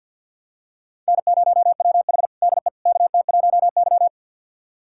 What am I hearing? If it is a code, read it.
Morse code, heard as a keyed tone, about 49 wpm, 700 Hz. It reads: N0WH DE KT1Y
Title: USA Callsign DE Callsign - 4 Characters - 50wpm